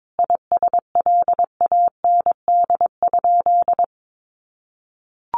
Morse code, heard as a keyed tone, about 22 wpm, 700 Hz. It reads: ISLAND?
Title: Sentences from Top 500 Words 22wpm